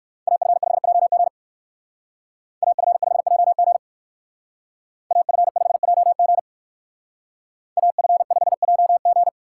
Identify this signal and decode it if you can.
Morse code, heard as a keyed tone, about 58 wpm, 700 Hz. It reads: AF5JG AF5JG AF5JG AF5JG